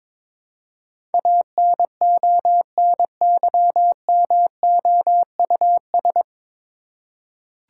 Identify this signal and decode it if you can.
Morse code, heard as a keyed tone, about 22 wpm, 700 Hz. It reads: ANONYMOUS
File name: 9 Letter Words - First 3 Letters Given - 22wpm